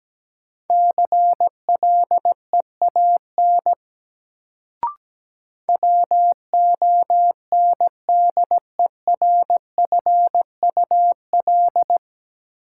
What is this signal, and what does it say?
Morse code, heard as a keyed tone, about 17 wpm, 700 Hz. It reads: CLEAN WONDERFUL
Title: Sentences from Top 1100 Words - New Word Drill - 17wpm